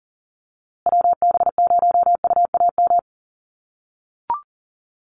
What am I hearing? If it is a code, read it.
Morse code, heard as a keyed tone, about 40 wpm, 700 Hz. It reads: WB0UAM
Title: USA Call Signs Repeated 2x 40wpm